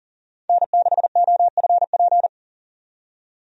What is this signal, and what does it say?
Morse code, heard as a keyed tone, about 40 wpm, 700 Hz. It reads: N6OFP
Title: USA Call Signs - 5 Characters - 40wpm